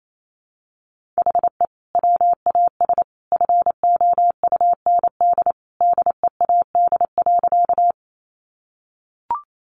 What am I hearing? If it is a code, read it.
Morse code, heard as a keyed tone, about 28 wpm, 700 Hz. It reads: HE WAS FOUND DEAD.
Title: Sentences from Top 1500 Words 28wpm